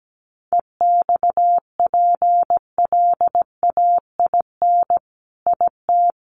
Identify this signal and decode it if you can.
Morse code, heard as a keyed tone, about 17 wpm, 700 Hz. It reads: EXPLAIN IT
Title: Sentences from Top 1100 Words 17wpm